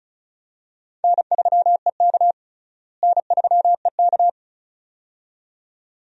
Morse code, heard as a keyed tone, about 36 wpm, 700 Hz. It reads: N3EK N3EK
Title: USA Call Signs - 4 Characters - Repeated 2x - 35wpm